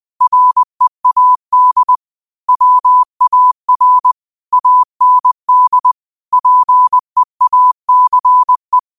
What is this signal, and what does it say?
Morse code, heard as a keyed tone, about 20 wpm, 1000 Hz. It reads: READ WAR AND PEACE
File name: qotdfiles / 20wpm